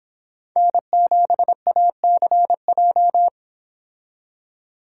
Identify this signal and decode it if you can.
Morse code, heard as a keyed tone, about 29 wpm, 700 Hz. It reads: N7ACJ